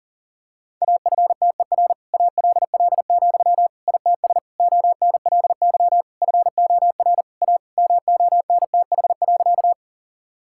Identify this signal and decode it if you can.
Morse code, heard as a keyed tone, about 37 wpm, 700 Hz. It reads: AFTER ALL, ITS ONLY FOR A MONTH.
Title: Sentences from Top 1000 Words - Review Part 5 - 40wpm